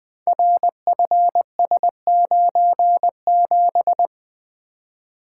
Morse code, heard as a keyed tone, about 20 wpm, 700 Hz. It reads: RFS97
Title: Code Groups - 5 Letter:Numbers - 20wpm